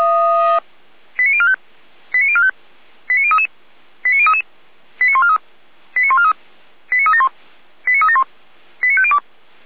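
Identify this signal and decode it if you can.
Five-tone selective call (ZVEI), heard as quick runs of five tones, tones of about 70 ms. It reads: 89045 89045 89033 89033 89143 89143 89371 89371 89502
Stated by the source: ZVEI five-tone call received by SDR.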